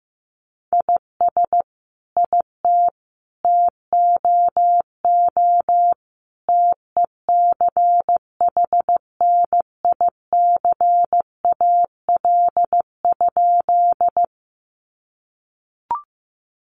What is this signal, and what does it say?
Morse code, heard as a keyed tone, about 15 wpm, 700 Hz. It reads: IS IT TOO TECHNICAL?
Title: Sentences from Top 1100 Words 15wpm